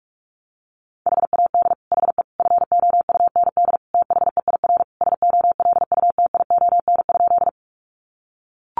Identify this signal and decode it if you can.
Morse code, heard as a keyed tone, about 44 wpm, 700 Hz. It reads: HAD HE FOUND THEIR SOLUTION?
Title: Sentences from Top 1300 Words 45wpm